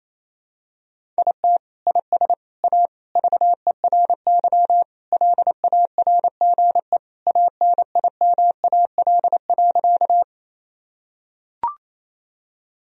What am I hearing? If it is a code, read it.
Morse code, heard as a keyed tone, about 28 wpm, 700 Hz. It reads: IT IS A VERY LARGE ANIMAL.
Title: Sentences from Top 1000 Words - Review Part 1 - 28wpm